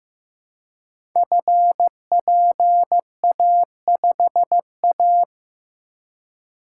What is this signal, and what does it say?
Morse code, heard as a keyed tone, about 15 wpm, 700 Hz. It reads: FPA5A
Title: Code Groups - 5 Letter:Numbers - 15wpm